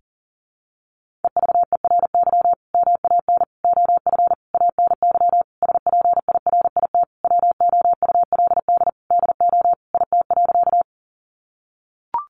Morse code, heard as a keyed tone, about 40 wpm, 700 Hz. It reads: EVERY MAN OF ANY SPIRIT WOULD DO IT.
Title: Sentences from Top 1400 Words 40wpm